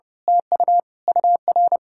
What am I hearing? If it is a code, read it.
Morse code, heard as a keyed tone, about 28 wpm, 700 Hz. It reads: TU UR
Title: Speed-Racing Common QSO Phrases - 30wpm